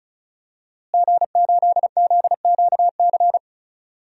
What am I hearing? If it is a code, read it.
Morse code, heard as a keyed tone, about 35 wpm, 700 Hz. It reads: G8ZQC
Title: DXCC-100 Call Signs - 5 Characters - Repeated 3x - 35wpm